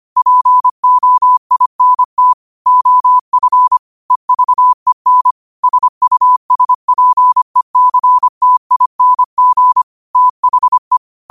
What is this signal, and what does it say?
Morse code, heard as a keyed tone, about 25 wpm, 1000 Hz. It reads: POINT OF EVEN SUSPECTING THE
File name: qotdfiles / 25wpm